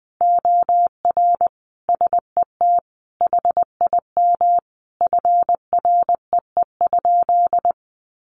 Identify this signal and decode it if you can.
Morse code, heard as a keyed tone, about 20 wpm, 700 Hz. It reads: OR SET HIM FREE?
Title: Sentences from Top 1100 Words 20wpm